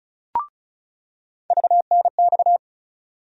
Morse code, heard as a keyed tone, about 36 wpm, 700 Hz. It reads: VNX
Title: USA Call Sign Postfixes - 3 Letters 35wpm